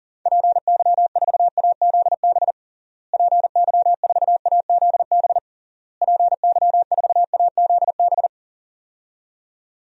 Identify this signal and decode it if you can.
Morse code, heard as a keyed tone, about 40 wpm, 700 Hz. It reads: PY4AZB PY4AZB PY4AZB
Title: DXCC-100 Call Signs - 6 Characters - Repeated 3x - 40wpm